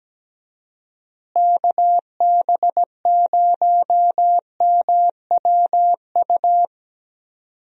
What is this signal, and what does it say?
Morse code, heard as a keyed tone, about 17 wpm, 700 Hz. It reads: KB0MWU
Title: USA Call Signs - 6 Characters - Repeated 3x - 17wpm